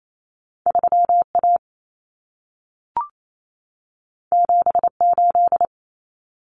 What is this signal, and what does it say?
Morse code, heard as a keyed tone, about 28 wpm, 700 Hz. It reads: 3A 78
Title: Common Cut Numbers - 2-Digit 28wpm